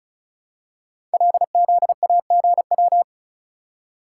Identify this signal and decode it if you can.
Morse code, heard as a keyed tone, about 36 wpm, 700 Hz. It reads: LZAGW
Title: Code Groups - 5 Letter:Numbers - 35wpm